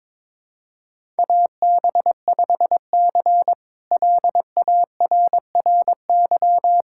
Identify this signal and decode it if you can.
Morse code, heard as a keyed tone, about 22 wpm, 700 Hz. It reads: AB5C LARRY
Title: K1USN SST QSO Practice - 22wpm